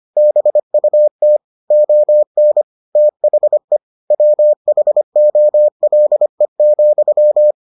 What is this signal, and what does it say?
Morse code, heard as a keyed tone, about 25 wpm, 600 Hz. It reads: BUT ON THE WHOLE,